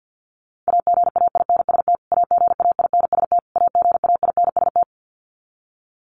Morse code, heard as a keyed tone, about 50 wpm, 700 Hz. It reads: AGAINST AGAINST AGAINST
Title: Top 500 Words Repeated 50wpm